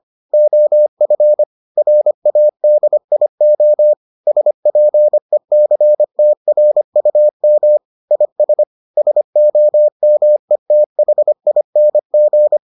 Morse code, heard as a keyed tone, about 25 wpm, 600 Hz. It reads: OF RADIO SPECTRUM IS SOMETHING